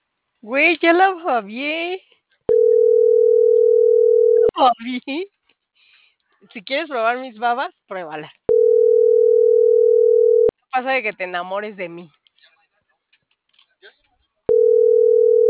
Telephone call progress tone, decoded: ringback tone

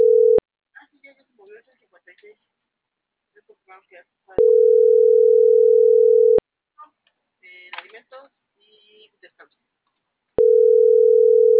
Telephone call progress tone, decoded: ringback tone

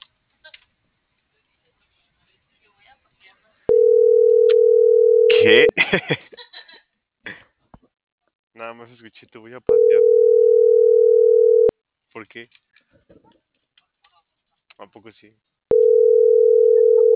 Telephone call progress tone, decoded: ringback tone

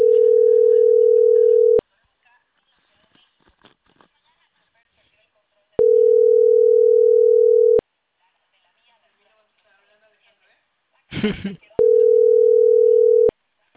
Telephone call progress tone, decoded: ringback tone

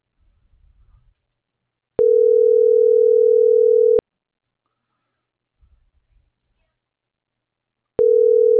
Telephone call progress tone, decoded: ringback tone

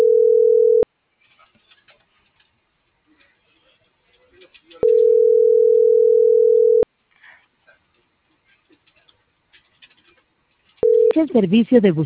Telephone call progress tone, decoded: ringback tone

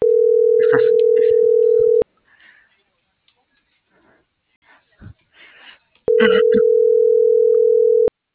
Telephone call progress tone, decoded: ringback tone

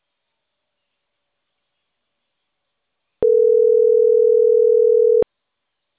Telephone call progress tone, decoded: ringback tone

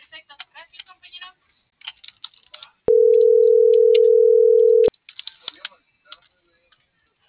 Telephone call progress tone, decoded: ringback tone

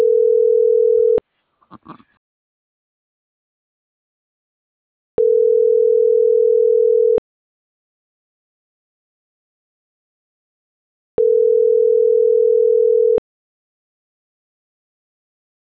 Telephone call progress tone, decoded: ringback tone

